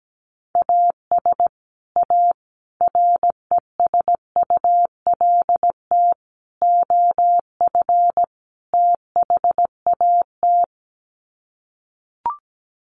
Morse code, heard as a keyed tone, about 17 wpm, 700 Hz. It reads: AS A RESULT OF THAT
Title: Sets of 5 Words Encore 17wpm